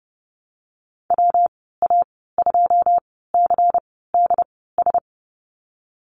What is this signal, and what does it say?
Morse code, heard as a keyed tone, about 30 wpm, spaced as for 17 wpm, 700 Hz. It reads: WA2CDS
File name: Lesson 19g - Letter w - Call Signs 30wpm